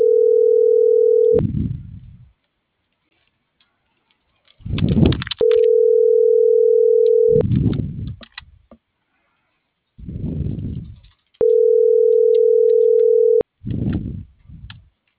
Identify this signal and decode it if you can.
Ringback tone (North America), 440 + 480 Hz.